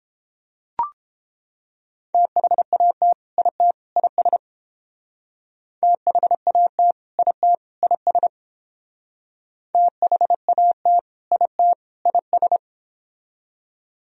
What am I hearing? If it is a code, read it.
Morse code, heard as a keyed tone, about 28 wpm, 700 Hz. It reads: THAT IT IS THAT IT IS THAT IT IS